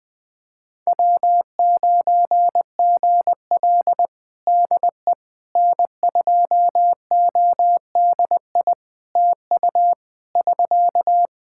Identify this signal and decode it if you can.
Morse code, heard as a keyed tone, about 20 wpm, 700 Hz. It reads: W9GL DE N2ODI TU <SK>
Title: Practice QSO - Exchange 4 - Ending - 20wpm